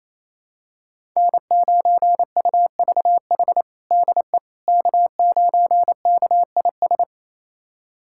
Morse code, heard as a keyed tone, about 27 wpm, 700 Hz. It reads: N9UVH DE K9KIS
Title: USA Callsign DE Callsign - 5 Characters - 28wpm